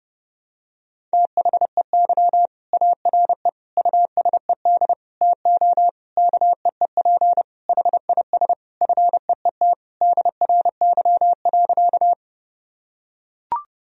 Morse code, heard as a keyed tone, about 30 wpm, 700 Hz. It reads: THEY ARE USED TO KEEP HIS FEET DRY.
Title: Sentences from Top 1000 Words - Review Part 3 - 30wpm